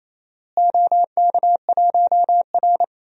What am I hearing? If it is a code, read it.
Morse code, heard as a keyed tone, about 28 wpm, 700 Hz. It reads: OK1R
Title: DXCC-100 Call Signs - 4 Characters - Repeated 2x - 28wpm